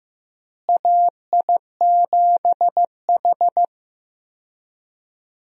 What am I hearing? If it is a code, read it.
Morse code, heard as a keyed tone, about 15 wpm, 700 Hz. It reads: AI7H
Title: USA Call Signs - 4 Characters - Repeated 2x - 15wpm